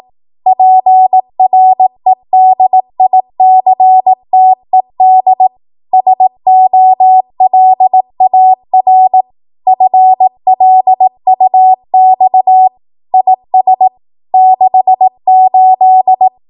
Morse code, heard as a keyed tone, about 18 wpm, 750 Hz. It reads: PREDICTED SOLAR FLUX IS 68
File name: Archive / 18 WPM / 18WPM